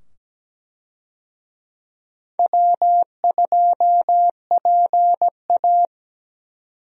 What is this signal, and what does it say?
Morse code, heard as a keyed tone, about 17 wpm, 700 Hz. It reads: W2PA
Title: USA Call Signs - 4 Characters - 17wpm